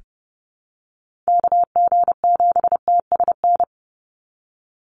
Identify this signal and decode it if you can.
Morse code, heard as a keyed tone, about 30 wpm, 700 Hz. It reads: KG7TSN
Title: USA Call Signs 30wpm